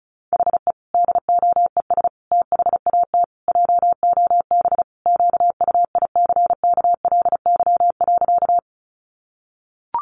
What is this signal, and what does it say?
Morse code, heard as a keyed tone, about 34 wpm, 700 Hz. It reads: HE DOES THAT JOB QUICKLY.
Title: Sentences from Top 700 Words 35wpm